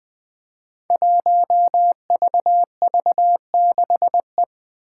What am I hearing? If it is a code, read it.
Morse code, heard as a keyed tone, about 20 wpm, 700 Hz. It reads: 1VV6E
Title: Five Challenging Letter-Number Combinations 20wpm